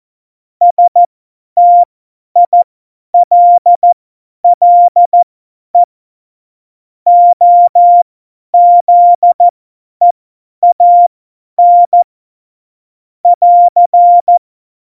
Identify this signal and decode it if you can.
Morse code, heard as a keyed tone, about 14 wpm, spaced as for 10 wpm, 700 Hz. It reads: STILLE OZEAN <AR>
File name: Morsekurs / Witze / Witz 04. Juni 2019, 50 F70